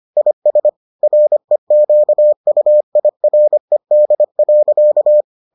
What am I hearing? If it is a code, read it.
Morse code, heard as a keyed tone, about 25 wpm, 600 Hz. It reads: IS REQUIRED.